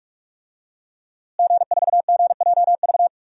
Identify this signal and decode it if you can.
Morse code, heard as a keyed tone, about 43 wpm, 700 Hz. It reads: G4GJV
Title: DXCC-100 Call Signs - 5 Characters - Repeated 3x - 45wpm